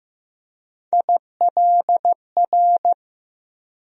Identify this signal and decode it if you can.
Morse code, heard as a keyed tone, about 15 wpm, 700 Hz. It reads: ILR
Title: USA Call Sign Postfixes - 3 Letters 15wpm